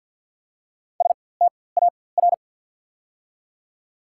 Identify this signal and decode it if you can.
Morse code, heard as a keyed tone, about 54 wpm, spaced as for 24 wpm, 700 Hz. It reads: STAR
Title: 4x Character Spacing - Top 500 Words - 50wpm